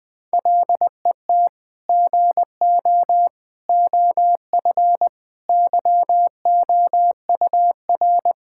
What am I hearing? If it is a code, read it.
Morse code, heard as a keyed tone, about 21 wpm, 700 Hz. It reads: LET GO OF YOUR
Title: Sentences from Top 1600 Words 20wpm